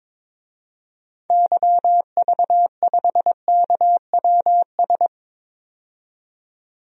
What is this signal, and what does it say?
Morse code, heard as a keyed tone, about 22 wpm, 700 Hz. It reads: YV5KWS
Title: DXCC-100 Call Signs - 6 Characters - 22wpm